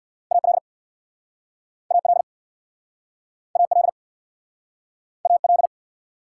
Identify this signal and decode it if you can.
Morse code, heard as a keyed tone, about 66 wpm, 690 Hz. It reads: AL AL AL AL